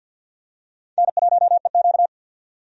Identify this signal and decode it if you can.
Morse code, heard as a keyed tone, about 52 wpm, 700 Hz. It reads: N1EQ
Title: USA Call Signs - 4 Characters - 50wpm